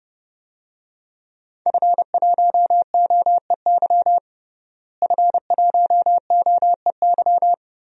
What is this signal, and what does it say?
Morse code, heard as a keyed tone, about 32 wpm, 700 Hz. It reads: F1OEY F1OEY
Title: DXCC-100 Call Signs - 5 Characters - Repeated 3x - 30wpm